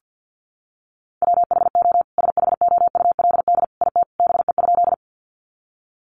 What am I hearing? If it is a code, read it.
Morse code, heard as a keyed tone, about 50 wpm, 700 Hz. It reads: WHO SHOULD IT BE?
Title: Sentences from Top 300 Words 50wpm